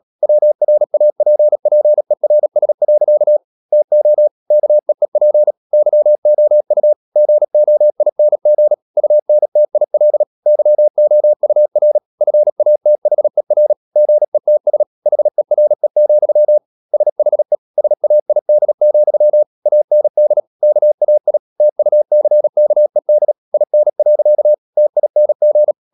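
Morse code, heard as a keyed tone, about 37 wpm, 600 Hz. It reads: WRAPPERS. TO KEEP YOU GOING UNTIL YOUR FATHER GETS HERE, SHE SAID, AND KAI TUCKED IN. TING
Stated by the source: Morse code practice at 35–39 wpm sending English text of Ken Follett's novel Never, from CW med Gnister.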